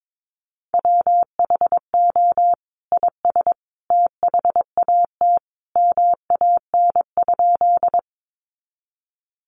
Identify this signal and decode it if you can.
Morse code, heard as a keyed tone, about 22 wpm, 700 Hz. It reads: WHO IS THAT MAN?
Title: Sentences from Top 200 Words 22wpm